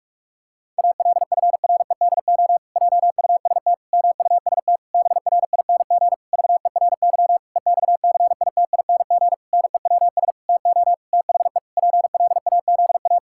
Morse code, heard as a keyed tone, about 44 wpm, 700 Hz. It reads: ALFREDO JUST MUST BRING VERY EXCITING NEWS TO THE PLAZA
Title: Pangram Challenge v2 - 45wpm